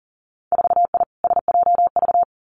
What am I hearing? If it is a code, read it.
Morse code, heard as a keyed tone, about 41 wpm, 700 Hz. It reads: 4I SJV
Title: Winter Field Day Exchanges Encore 40wpm